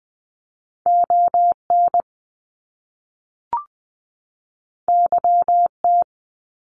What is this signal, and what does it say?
Morse code, heard as a keyed tone, about 20 wpm, 700 Hz. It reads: ON YT
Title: Canadian Provinces and Territories 20wpm